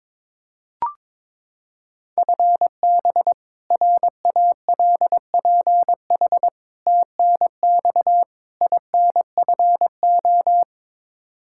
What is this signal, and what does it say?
Morse code, heard as a keyed tone, about 22 wpm, 700 Hz. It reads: FB RALPH TNX INFO
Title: Practice QSO - Exchange 3 - More Info - 22wpm